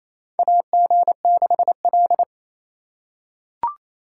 Morse code, heard as a keyed tone, about 28 wpm, 700 Hz. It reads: AG6L
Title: USA Call Signs - 3x Character Spacing - 28wpm